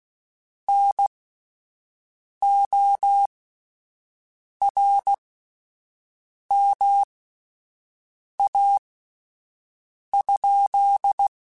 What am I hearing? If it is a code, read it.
Morse code, heard as a keyed tone, about 16 wpm, spaced as for 5 wpm, 800 Hz. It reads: NORMA?
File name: QSOs / QSOs-mp3s / WD0FBT 1@10